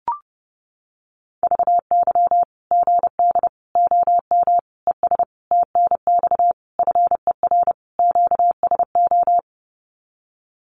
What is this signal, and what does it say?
Morse code, heard as a keyed tone, about 30 wpm, 700 Hz. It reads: VY GD OM ES TNX FER QSO
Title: Common QSO Phrases 30wpm